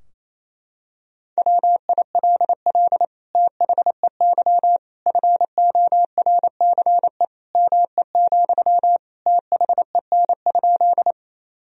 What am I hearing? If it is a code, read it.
Morse code, heard as a keyed tone, about 27 wpm, 700 Hz. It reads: WILL THEY FORCE ME, THEN?
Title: Sentences from Top 1000 Words - Review Part 3 - 28wpm